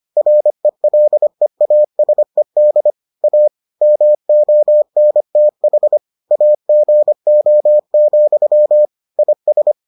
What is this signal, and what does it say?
Morse code, heard as a keyed tone, about 25 wpm, 600 Hz. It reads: RELEASED A MONTH AGO, IS